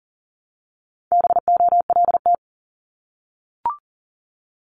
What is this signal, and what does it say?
Morse code, heard as a keyed tone, about 38 wpm, 700 Hz. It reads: BOLT
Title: Multi-letter Phonemes Repeated 3x 40wpm